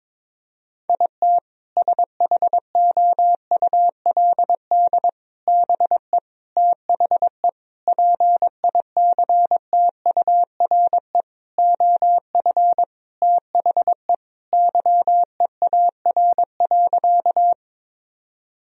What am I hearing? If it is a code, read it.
Morse code, heard as a keyed tone, about 22 wpm, 700 Hz. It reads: IT SHOULD BE THE PICTURE OF THE YEAR.
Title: Sentences from Top 1000 Words - Review Part 1 - 22wpm